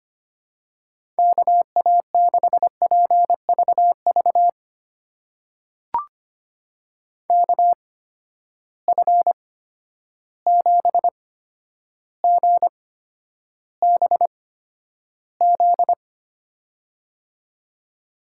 Morse code, heard as a keyed tone, about 25 wpm, spaced as for 11 wpm, 700 Hz. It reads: KA6PVV KF7GBZ
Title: USA Call Signs - 6 Characters - 8x Character Spacing - 25wpm